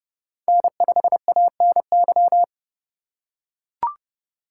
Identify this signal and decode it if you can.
Morse code, heard as a keyed tone, about 29 wpm, 700 Hz. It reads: N5ANY
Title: USA Call Signs - 3x Character Spacing - 30wpm